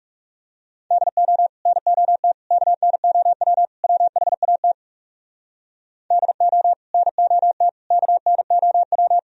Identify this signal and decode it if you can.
Morse code, heard as a keyed tone, about 37 wpm, 700 Hz. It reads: DO NOT KNOW WHAT DO NOT KNOW
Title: Speed-Racing - Sets of 4 Words 30wpm